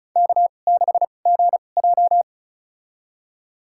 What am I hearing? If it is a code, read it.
Morse code, heard as a keyed tone, about 34 wpm, spaced as for 25 wpm, 700 Hz. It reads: K6GJ
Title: USA Call Signs - 2x Character Spacing - 35wpm